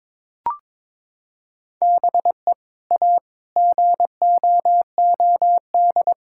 Morse code, heard as a keyed tone, about 21 wpm, 700 Hz. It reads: BE A GOOD
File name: practice / 22wpm / Sentences with 2-Word Phrases - 22wpm